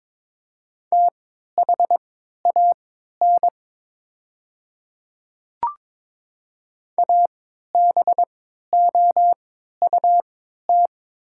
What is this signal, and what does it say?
Morse code, heard as a keyed tone, about 22 wpm, spaced as for 12 wpm, 700 Hz. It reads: THAN ABOUT
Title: 3x Character Spacing - Top 500 Words - 22wpm